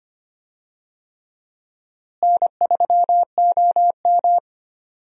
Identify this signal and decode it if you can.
Morse code, heard as a keyed tone, about 25 wpm, 700 Hz. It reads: N3OM
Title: USA Call Signs - 4 Characters - Repeated 3x - 25wpm